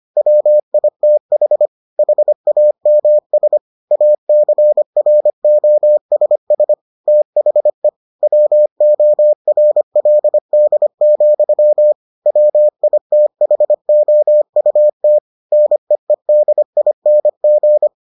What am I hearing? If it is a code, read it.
Morse code, heard as a keyed tone, about 25 wpm, 600 Hz. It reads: WITH HAMS ACROSS THE WORLD, WITHOUT NEEDING